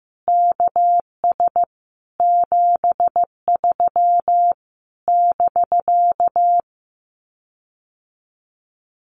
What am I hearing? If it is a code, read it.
Morse code, heard as a keyed tone, about 15 wpm, 700 Hz. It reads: KS 73 <BK>
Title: Common QSO Phrases Encore 15wpm